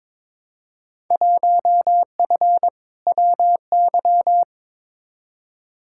Morse code, heard as a keyed tone, about 22 wpm, 700 Hz. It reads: 1F WY
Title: Field Day Exchanges Encore - 22wpm